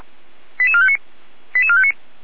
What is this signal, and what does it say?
Five-tone selective call (ZVEI), heard as quick runs of five tones, tones of about 80 ms. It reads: 88459 88459
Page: SDR/ZVEI